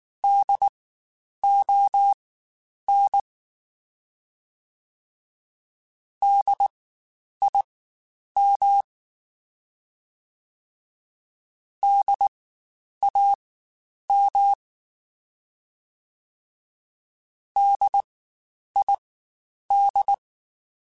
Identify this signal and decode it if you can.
Morse code, heard as a keyed tone, about 19 wpm, spaced as for 9 wpm, 800 Hz. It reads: DON DIM DAM DID
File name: Spaced Alphas / Spaced Alphas mp3s / Ds sp @18